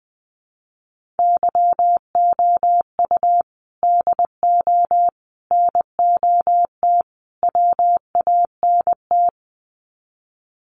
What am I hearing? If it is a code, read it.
Morse code, heard as a keyed tone, about 20 wpm, 700 Hz. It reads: YOU DO NOT WANT
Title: Sets of 4 Words 20wpm